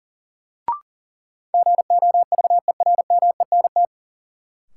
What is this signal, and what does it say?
Morse code, heard as a keyed tone, about 40 wpm, 700 Hz. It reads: GOVERMENT
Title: Sentences from Top 1300 Words - New Word Drill - 40wpm